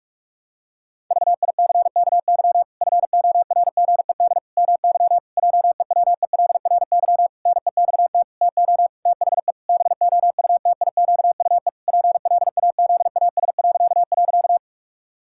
Morse code, heard as a keyed tone, about 44 wpm, 700 Hz. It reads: VICKY FORGED MY JEWELRY NEXT TO THE BOUTIQUE PLAZAS..
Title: Pangram Challenge v2 - 45wpm